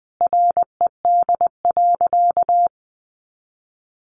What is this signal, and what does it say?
Morse code, heard as a keyed tone, about 20 wpm, 700 Hz. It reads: RED.
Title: Additional Easy Phrases 20wpm